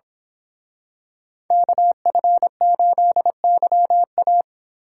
Morse code, heard as a keyed tone, about 26 wpm, 700 Hz. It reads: KF8YA
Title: Speed Racing - USA Call Signs 22wpm